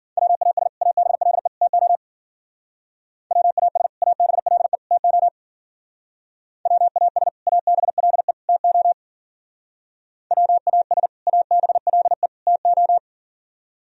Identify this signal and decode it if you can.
Morse code, heard as a keyed tone, about 54 wpm, 700 Hz. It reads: WAS ABLE TO WAS ABLE TO WAS ABLE TO WAS ABLE TO